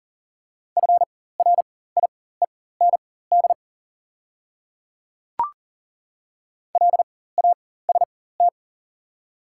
Morse code, heard as a keyed tone, about 40 wpm, spaced as for 19 wpm, 700 Hz. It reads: FRIEND LAST
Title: 4x Character Spacing - Top 300 Words - 40wpm